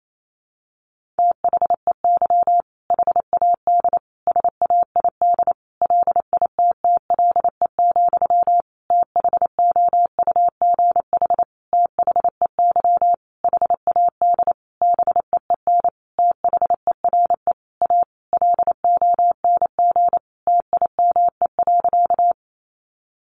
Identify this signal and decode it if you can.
Morse code, heard as a keyed tone, about 28 wpm, 700 Hz. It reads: THEY HAD SAID LITTLE, THOUGH THEY HAD BEEN THERE A LONG TIME.
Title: Sentences from Top 400 Words 28wpm